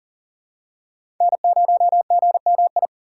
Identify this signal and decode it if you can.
Morse code, heard as a keyed tone, about 40 wpm, 700 Hz. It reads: N0GMI
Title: USA Call Signs - 2x Character Spacing - 40wpm